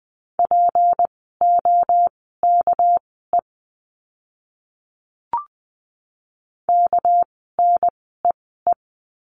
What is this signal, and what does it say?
Morse code, heard as a keyed tone, about 20 wpm, spaced as for 14 wpm, 700 Hz. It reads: POKE KNEE
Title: 199 Lesson 32f - Letter k - Words 20wpm